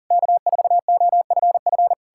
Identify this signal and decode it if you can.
Morse code, heard as a keyed tone, about 52 wpm, 700 Hz. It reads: K4OFF